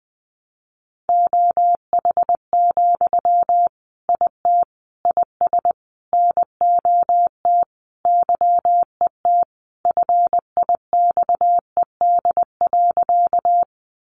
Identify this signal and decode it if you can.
Morse code, heard as a keyed tone, about 20 wpm, 700 Hz. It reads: OH, IT IS NOT YET FIXED.
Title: Sentences from Top 1100 Words 20wpm